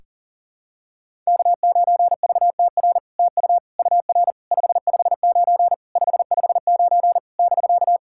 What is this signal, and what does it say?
Morse code, heard as a keyed tone, about 44 wpm, 690 Hz. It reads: K9VTR TU UR 559 559 <BK>
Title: SOTA QSO Practice 40wpm